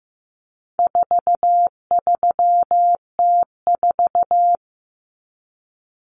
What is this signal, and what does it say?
Morse code, heard as a keyed tone, about 15 wpm, 700 Hz. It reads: 43T4
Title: Common Cut Numbers - 4-Digit 15wpm